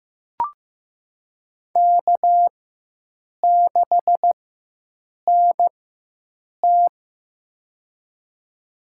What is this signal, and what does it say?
Morse code, heard as a keyed tone, about 15 wpm, spaced as for 7 wpm, 700 Hz. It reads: K6NT
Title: USA Call Signs - 4 Characters - 4x Character Spacing - 15wpm